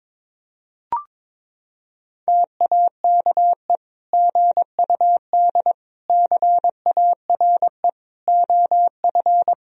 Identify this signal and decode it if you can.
Morse code, heard as a keyed tone, about 22 wpm, 700 Hz. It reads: TAKE GUD CARE OF
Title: Sentences from Top 400 Words and Abbreviations 22wpm